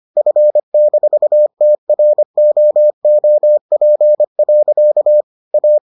Morse code, heard as a keyed tone, about 25 wpm, 600 Hz. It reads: F-TROOP. A